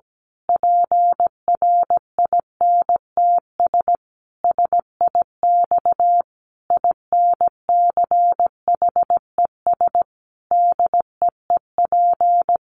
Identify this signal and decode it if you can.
Morse code, heard as a keyed tone, about 17 wpm, 700 Hz. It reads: PRINTS SIX INCHES DEEP
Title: A Sound Of Thunder 17wpm